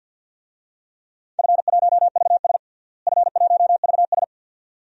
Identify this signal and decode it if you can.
Morse code, heard as a keyed tone, about 49 wpm, 700 Hz. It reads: F1VS F1VS
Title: DXCC-100 Call Signs - 4 Characters - Repeated 2x - 50wpm